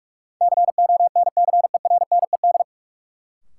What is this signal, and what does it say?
Morse code, heard as a keyed tone, about 43 wpm, 700 Hz. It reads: CONCERNED
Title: Sentences from Top 1000 Words - Review Part 6 - New Word Drill - 45wpm